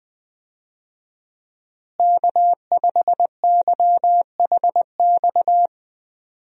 Morse code, heard as a keyed tone, about 21 wpm, 700 Hz. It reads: K5YHX
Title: USA Call Signs - 5 Characters - Repeated 3x - 20wpm